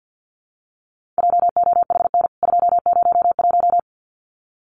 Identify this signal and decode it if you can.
Morse code, heard as a keyed tone, about 48 wpm, 700 Hz. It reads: JOHN 201
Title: CWT Exchange 50wpm